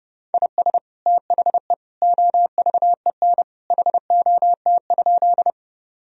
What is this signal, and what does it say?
Morse code, heard as a keyed tone, about 30 wpm, 700 Hz. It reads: IS THE OVEN HOT?
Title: Sentences from Top 1000 Words - Review Part 4 - 30wpm